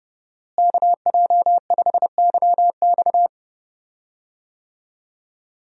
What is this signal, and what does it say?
Morse code, heard as a keyed tone, about 30 wpm, 700 Hz. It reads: KJ5YX
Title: USA Call Signs - 5 Characters - Repeated 2x - 30wpm